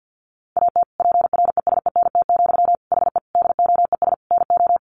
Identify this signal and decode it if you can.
Morse code, heard as a keyed tone, about 50 wpm, 700 Hz. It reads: AT PRESENT, HE DOES NO
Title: Sentences from Top 800 Words 50wpm